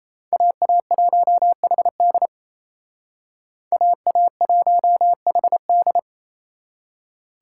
Morse code, heard as a keyed tone, about 35 wpm, 700 Hz. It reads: AA1HD AA1HD